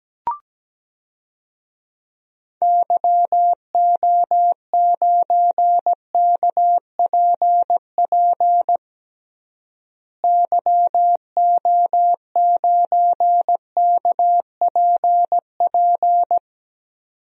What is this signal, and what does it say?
Morse code, heard as a keyed tone, about 17 wpm, 700 Hz. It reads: YO9KPP YO9KPP
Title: DXCC-100 Call Signs - 6 Characters - Repeated 2x - 17wpm